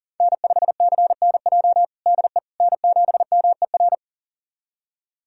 Morse code, heard as a keyed tone, about 40 wpm, 700 Hz. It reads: N5CNJ DE N7MER
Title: USA Callsign DE Callsign - 5 Characters - 40wpm